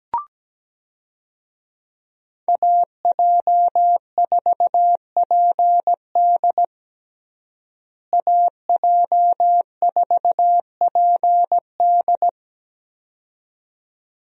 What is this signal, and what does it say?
Morse code, heard as a keyed tone, about 17 wpm, 700 Hz. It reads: AJ4PD AJ4PD
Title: USA Call Signs - 5 Characters - Repeated 2x - 17wpm